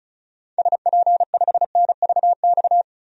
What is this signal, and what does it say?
Morse code, heard as a keyed tone, about 36 wpm, 700 Hz. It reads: SP5NVX
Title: DXCC-100 Call Signs - 6 Characters - 35wpm